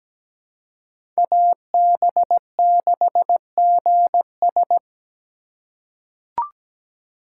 Morse code, heard as a keyed tone, about 17 wpm, 700 Hz. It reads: AB6GS
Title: USA Call Signs - 5 Characters - 4x Character Spacing - 17wpm